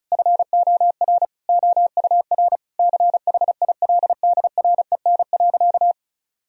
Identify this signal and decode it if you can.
Morse code, heard as a keyed tone, about 36 wpm, 700 Hz. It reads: FOR OUR CHILDREN.
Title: Sentences from Top 1000 Words - Review Part 4 - 35wpm